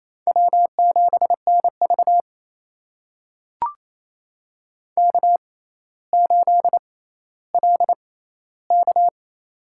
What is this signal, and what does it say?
Morse code, heard as a keyed tone, about 28 wpm, spaced as for 14 wpm, 700 Hz. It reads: W7NV K8LK
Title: USA Call Signs - 6x Character Spacing - 28wpm